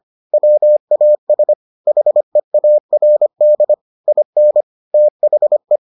Morse code, heard as a keyed tone, about 25 wpm, 600 Hz. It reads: WAS HEARD IN THE